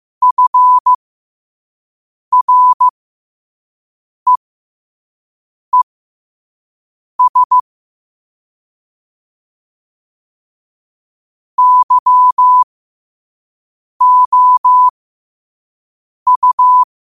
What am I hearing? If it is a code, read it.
Morse code, heard as a keyed tone, about 15 wpm, spaced as for 5 wpm, 1000 Hz. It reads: FREES YOU
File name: qotdfiles / 5wpm